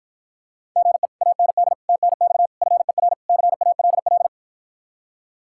Morse code, heard as a keyed tone, about 52 wpm, 700 Hz. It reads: GE AND TNX FER CALL